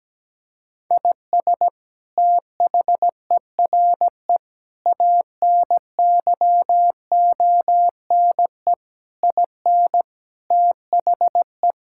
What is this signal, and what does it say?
Morse code, heard as a keyed tone, about 17 wpm, 700 Hz. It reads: IS THERE ANYONE IN THE
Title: Sentences from Top 1000 Words - Review Part 5 - 17wpm